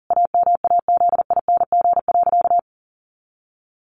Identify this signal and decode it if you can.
Morse code, heard as a keyed tone, about 40 wpm, 700 Hz. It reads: AMAZING.
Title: Sentences from Top 1500 Words 40wpm